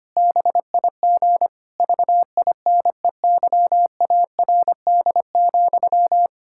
Morse code, heard as a keyed tone, about 25 wpm, 700 Hz. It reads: BIG VINEYARD,
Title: Pangram Challenge v2 - 25wpm